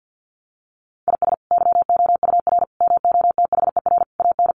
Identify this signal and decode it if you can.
Morse code, heard as a keyed tone, about 50 wpm, 700 Hz. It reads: IS YOUR MOTHER AN
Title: Sentences from Top 1500 Words 50wpm